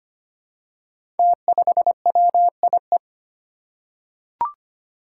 Morse code, heard as a keyed tone, about 25 wpm, 700 Hz. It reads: T5WIE
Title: Five Challenging Letter-Number Combinations 25wpm